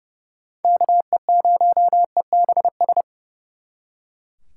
Morse code, heard as a keyed tone, about 36 wpm, 700 Hz. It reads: KE0EBS
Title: Speed Racing - USA 2x3 Call Signs 30wpm